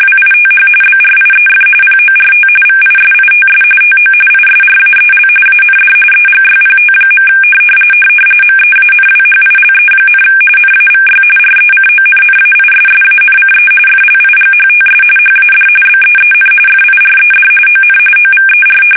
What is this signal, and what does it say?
RTTY, about 45.45 baud. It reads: WITH BRIEF PORT VISITS TO ISRAEL AND EGYPT. MEANWHILE, NAVY RECONNAISSANCE FLIGHTS HAD BEEN FIRED ON OVER
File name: fsk850 / battleshipnj